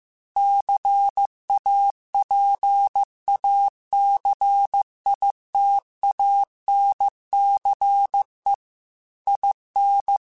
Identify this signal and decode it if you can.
Morse code, heard as a keyed tone, about 15 wpm, 800 Hz. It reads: CAPACITANCE IN